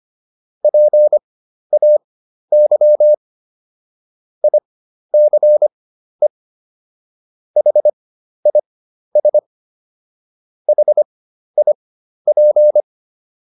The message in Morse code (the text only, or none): PAY ICE HIS HIP